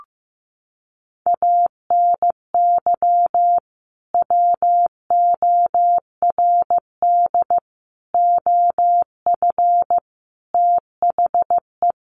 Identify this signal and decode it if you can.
Morse code, heard as a keyed tone, about 15 wpm, 700 Hz. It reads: ANY WORD OF THE
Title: Sentences from Top 1300 Words 15wpm